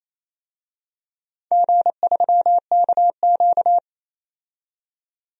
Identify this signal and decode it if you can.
Morse code, heard as a keyed tone, about 28 wpm, 700 Hz. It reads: G3KQ
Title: DXCC-100 Call Signs - 4 Characters - 28wpm